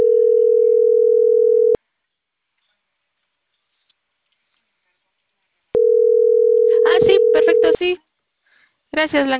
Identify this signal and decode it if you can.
Ringback tone (North America), 440 + 480 Hz, with speech.
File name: CENTRAL CAEA / out agcaea01 - .60301